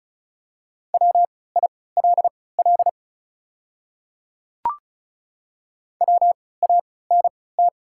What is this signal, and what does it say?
Morse code, heard as a keyed tone, about 36 wpm, spaced as for 20 wpm, 690 Hz. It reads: WILL WANT
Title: 3x Character Spacing - Top 300 Words - 35wpm